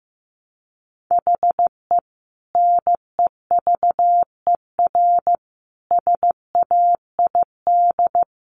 Morse code, heard as a keyed tone, about 15 wpm, 700 Hz. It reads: HE NEVER SAID
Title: Sentences from Top 500 Words 15wpm